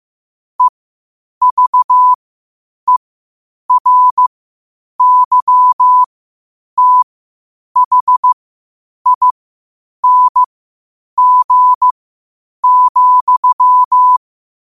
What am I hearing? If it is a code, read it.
Morse code, heard as a keyed tone, about 15 wpm, spaced as for 8 wpm, 1000 Hz. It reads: EVERYTHING,
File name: 8wpm